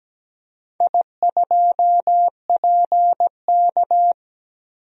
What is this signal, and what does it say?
Morse code, heard as a keyed tone, about 17 wpm, 700 Hz. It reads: I2PK
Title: DXCC-100 Call Signs - 4 Characters - Repeated 3x - 17wpm